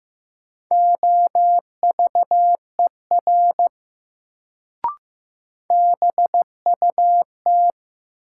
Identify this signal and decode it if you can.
Morse code, heard as a keyed tone, about 15 wpm, 700 Hz. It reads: OVER BUT
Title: Sentences from Top 100 Words - New Word Drill - 15wpm